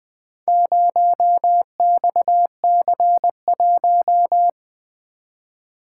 Morse code, heard as a keyed tone, about 20 wpm, 700 Hz. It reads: 0XC1
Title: Four Letter-Number Combinations - 20wpm